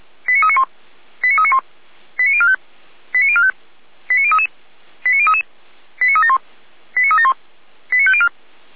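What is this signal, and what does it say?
Five-tone selective call (ZVEI), heard as quick runs of five tones, tones of about 80 ms. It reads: 89391 89391 89045 89045 89033 89033 89371 89371 89504
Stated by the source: ZVEI five-tone call received on SDR.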